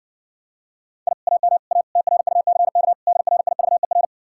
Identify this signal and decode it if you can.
Morse code, heard as a keyed tone, about 60 wpm, 700 Hz. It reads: I AM A TRUCK DRIVER